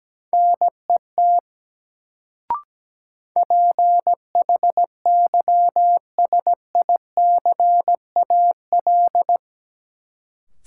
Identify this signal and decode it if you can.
Morse code, heard as a keyed tone, about 17 wpm, 700 Hz. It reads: NET PHYSICAL
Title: Sentences from Top 1400 Words - New Word Drill Alt - 17wpm